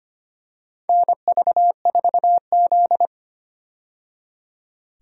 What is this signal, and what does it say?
Morse code, heard as a keyed tone, about 25 wpm, 700 Hz. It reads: NV4Z
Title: USA Call Signs - 4 Characters - 25wpm